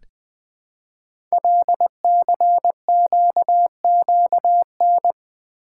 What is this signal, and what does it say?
Morse code, heard as a keyed tone, about 20 wpm, 700 Hz. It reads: LCQQN
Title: Code Groups - 5 Letters - 20wpm